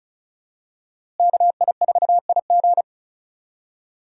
Morse code, heard as a keyed tone, about 34 wpm, 700 Hz. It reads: KI4IG
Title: USA Call Signs - 5 Characters - Repeated 3x - 35wpm